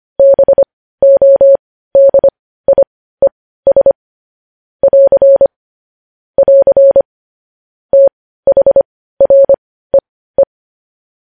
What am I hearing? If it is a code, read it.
Morse code, heard as a keyed tone, about 25 wpm, spaced as for 15 wpm, 550 Hz. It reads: BODIES <AR> <AR> THREE
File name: podcast / News headlines-1x-25x15